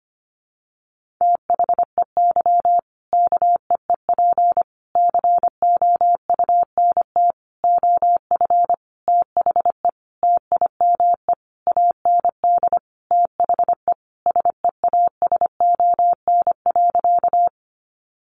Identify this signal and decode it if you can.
Morse code, heard as a keyed tone, about 25 wpm, 700 Hz. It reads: THEY KEEP COUNT OF THE TIME AND THE SEASON.
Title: Sentences from Top 1500 Words 25wpm